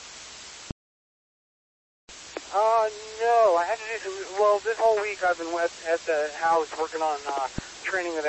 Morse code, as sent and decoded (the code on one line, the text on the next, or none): none